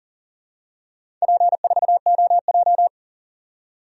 Morse code, heard as a keyed tone, about 40 wpm, 700 Hz. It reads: P4OJ
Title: DXCC-100 Call Signs - 4 Characters - 40wpm